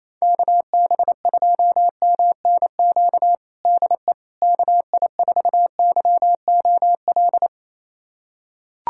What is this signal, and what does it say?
Morse code, heard as a keyed tone, about 28 wpm, 700 Hz. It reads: KB2MNQ DE KI4YOL
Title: USA Callsign DE Callsign - 6 Characters - 28wpm